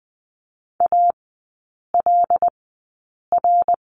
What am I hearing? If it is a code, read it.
Morse code, heard as a keyed tone, about 20 wpm, spaced as for 8 wpm, 700 Hz. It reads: ALR
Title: 178 Lesson 29e - Slash - Three Characters 20wpm